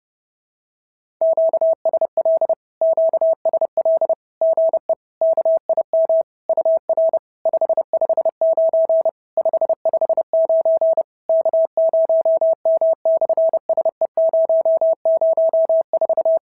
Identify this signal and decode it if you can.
Morse code, heard as a keyed tone, about 30 wpm, 650 Hz. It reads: QSL QSL GE KIM UR 559 559 K0M/SE004
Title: SOTA QSO Practice 30wpm